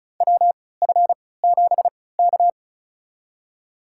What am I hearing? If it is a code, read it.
Morse code, heard as a keyed tone, about 36 wpm, spaced as for 20 wpm, 700 Hz. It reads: WF7K
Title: USA Call Signs - 4 Characters - 3x Character Spacing - 35wpm